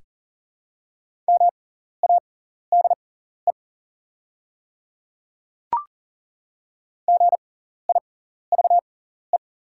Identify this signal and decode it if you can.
Morse code, heard as a keyed tone, about 40 wpm, spaced as for 14 wpm, 700 Hz. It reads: MADE GIVE
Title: 6x Character Spacing - Top 300 Words - 40wpm